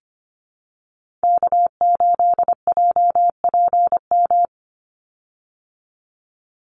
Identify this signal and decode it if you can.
Morse code, heard as a keyed tone, about 25 wpm, 700 Hz. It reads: K8JPM
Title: Call Signs Repeated 2x - 25wpm